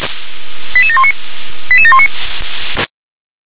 Five-tone selective call (ZVEI), heard as quick runs of five tones, tones of about 70 ms. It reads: 88719 88719